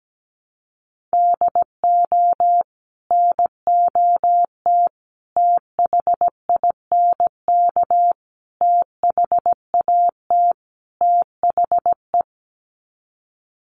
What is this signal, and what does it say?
Morse code, heard as a keyed tone, about 17 wpm, 700 Hz. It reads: DO NOT THINK THAT THE
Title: Sets of 5 Words Encore2 17wpm